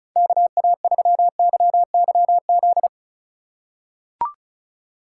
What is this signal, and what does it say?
Morse code, heard as a keyed tone, about 36 wpm, 700 Hz. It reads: KA3YYZ
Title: USA Call Signs - 6 Characters - 2x Character Spacing - 35wpm